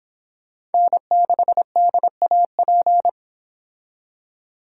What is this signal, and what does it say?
Morse code, heard as a keyed tone, about 30 wpm, 700 Hz. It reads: N6DAP